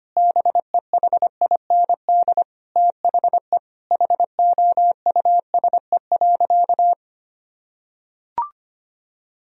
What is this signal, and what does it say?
Morse code, heard as a keyed tone, about 25 wpm, 700 Hz. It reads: BEHIND THE HOUSE.
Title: Sentences from Top 1000 Words - Review Part 4 - 25wpm